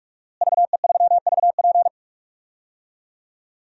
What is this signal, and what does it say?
Morse code, heard as a keyed tone, about 46 wpm, 700 Hz. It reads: VE3VP
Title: DXCC-100 Call Signs - 5 Characters - Repeated 2x - 45wpm